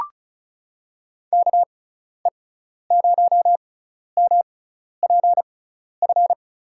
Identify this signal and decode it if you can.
Morse code, heard as a keyed tone, about 36 wpm, spaced as for 12 wpm, 700 Hz. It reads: KE0MPF
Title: USA Call Signs - 6 Characters - 6x Character Spacing - 35wpm